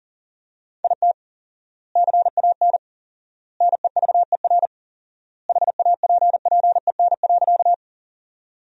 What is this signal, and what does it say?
Morse code, heard as a keyed tone, about 40 wpm, 700 Hz. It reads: IT CAN NEVER HAPPEN.
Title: 4x Word-Spacing - Sentences from Top 400 Words - 40wpm